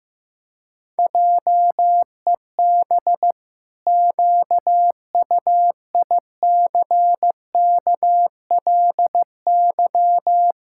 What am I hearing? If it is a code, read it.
Morse code, heard as a keyed tone, about 15 wpm, 700 Hz. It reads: JEB QUICKLY
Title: Pangram Challenge v2 - 15wpm